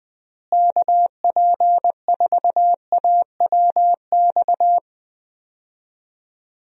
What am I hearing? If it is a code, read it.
Morse code, heard as a keyed tone, about 21 wpm, 700 Hz. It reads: KP4AWX
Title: DXCC-100 Call Signs - 6 Characters - Repeated 3x - 20wpm